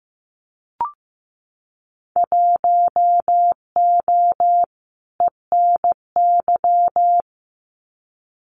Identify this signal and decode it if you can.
Morse code, heard as a keyed tone, about 15 wpm, 700 Hz. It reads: 1O ENY
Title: Winter Field Day Exchanges 15wpm